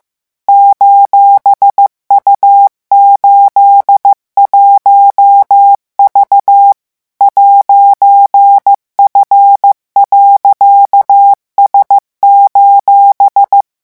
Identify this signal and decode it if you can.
Morse code, heard as a keyed tone, about 15 wpm, 800 Hz. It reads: :U81V 'F.S: